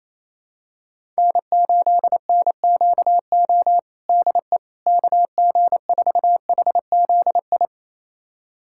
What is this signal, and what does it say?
Morse code, heard as a keyed tone, about 28 wpm, 700 Hz. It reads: N8NQO DE KG4HZI
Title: USA Callsign DE Callsign 28wpm